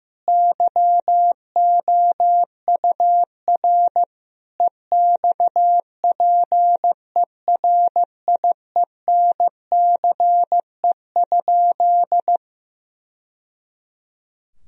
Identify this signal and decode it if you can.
Morse code, heard as a keyed tone, about 15 wpm, 700 Hz. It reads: YOUR EXPERIENCE?
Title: Sentences from Top 1000 Words - Review Part 4 - 15wpm